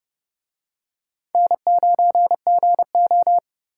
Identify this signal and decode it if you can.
Morse code, heard as a keyed tone, about 30 wpm, 700 Hz. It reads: N9GO